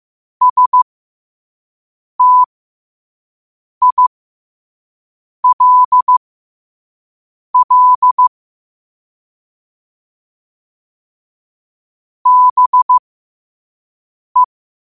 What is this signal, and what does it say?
Morse code, heard as a keyed tone, about 15 wpm, spaced as for 5 wpm, 1000 Hz. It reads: STILL BE
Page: New quotes every day in morse code at 5 Words per minute.